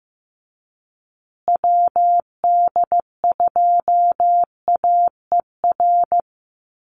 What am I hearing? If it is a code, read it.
Morse code, heard as a keyed tone, about 15 wpm, 700 Hz. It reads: WD2AER
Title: USA Call Signs Repeated 3x 15wpm